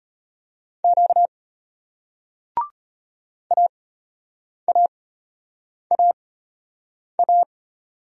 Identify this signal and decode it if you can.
Morse code, heard as a keyed tone, about 33 wpm, spaced as for 8 wpm, 700 Hz. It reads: Q AAAA